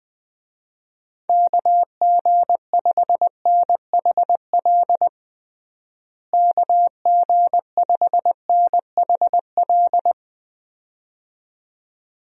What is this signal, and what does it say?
Morse code, heard as a keyed tone, about 21 wpm, 700 Hz. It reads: KG5NHL KG5NHL
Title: USA Call Signs - 6 Characters - Repeated 2x - 20wpm